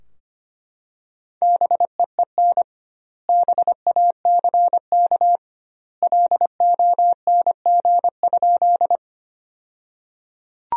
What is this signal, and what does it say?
Morse code, heard as a keyed tone, about 25 wpm, 700 Hz. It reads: BEEN BACK LONG?
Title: 2x Word-Spacing - Sentences from Top 200 Words - 25wpm